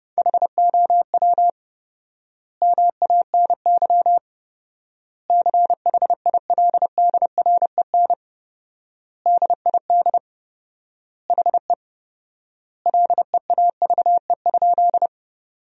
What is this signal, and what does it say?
Morse code, heard as a keyed tone, about 32 wpm, 700 Hz. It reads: HOW MANY CHILDREN DID HE LEAVE?
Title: 4x Word-Spacing - Sentences from Top 400 Words - 30wpm